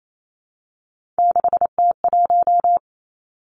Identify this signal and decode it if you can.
Morse code, heard as a keyed tone, about 28 wpm, 700 Hz. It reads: 6T1
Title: Three Challenging Letter-Number Combinations 28wpm